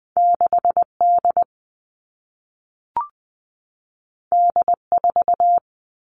Morse code, heard as a keyed tone, about 20 wpm, 700 Hz. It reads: 6 D D 4
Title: All Cut Numbers - 2-Digit 20wpm